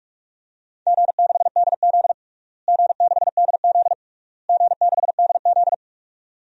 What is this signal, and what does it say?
Morse code, heard as a keyed tone, about 47 wpm, 700 Hz. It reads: G6DZ G6DZ G6DZ
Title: DXCC-100 Call Signs - 4 Characters - Repeated 3x - 45wpm